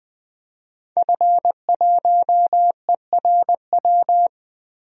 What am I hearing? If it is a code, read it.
Morse code, heard as a keyed tone, about 20 wpm, 700 Hz. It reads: F1ERW
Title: DXCC-100 Call Signs - 5 Characters - Repeated 2x - 20wpm